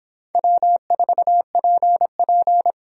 Morse code, heard as a keyed tone, about 25 wpm, 700 Hz. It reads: W4PP